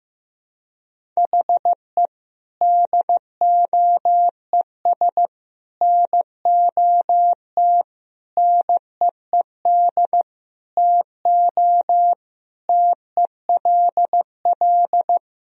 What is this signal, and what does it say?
Morse code, heard as a keyed tone, about 15 wpm, 700 Hz. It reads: HE DOES NOT NEED TO TELL
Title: Sentences from Top 1000 Words - Review Part 1 - 15wpm